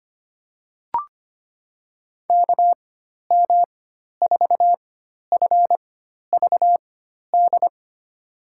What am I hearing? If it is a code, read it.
Morse code, heard as a keyed tone, about 25 wpm, spaced as for 12 wpm, 700 Hz. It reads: KM4FVD
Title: USA Call Signs - 6 Characters - 4x Character Spacing - 25wpm